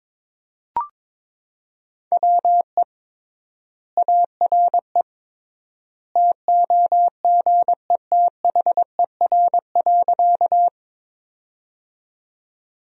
Morse code, heard as a keyed tone, about 22 wpm, 700 Hz. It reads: WE ARE TOGETHER.
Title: 3x Word-Spacing - Sentences from Top 300 Words - 22wpm